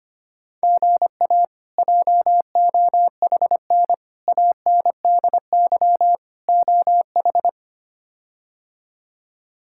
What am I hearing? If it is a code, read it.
Morse code, heard as a keyed tone, about 25 wpm, 700 Hz. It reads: GA JOHN ANDY OH
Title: K1USN SST QSO Practice - 25wpm